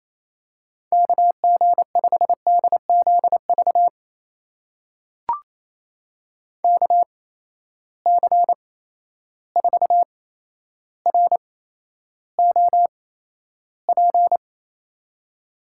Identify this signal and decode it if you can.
Morse code, heard as a keyed tone, about 27 wpm, spaced as for 12 wpm, 700 Hz. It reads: KG5DZV KC4ROP
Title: USA Call Signs - 6 Characters - 8x Character Spacing - 28wpm